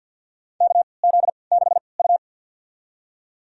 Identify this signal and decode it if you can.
Morse code, heard as a keyed tone, about 52 wpm, spaced as for 29 wpm, 700 Hz. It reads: KZ6U
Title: USA Call Signs - 3x Character Spacing - 50wpm